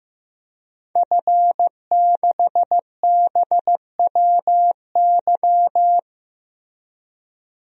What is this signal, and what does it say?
Morse code, heard as a keyed tone, about 15 wpm, 700 Hz. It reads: F6BWY
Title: DXCC-100 Call Signs - 5 Characters - 15wpm